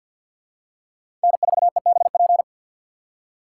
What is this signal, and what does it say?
Morse code, heard as a keyed tone, about 53 wpm, 700 Hz. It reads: N4EBP
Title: USA Call Signs - 4x Character Spacing - 50wpm